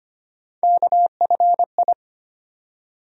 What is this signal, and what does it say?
Morse code, heard as a keyed tone, about 25 wpm, 700 Hz. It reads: KFI
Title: USA Call Sign Postfixes - 3 Letters 25wpm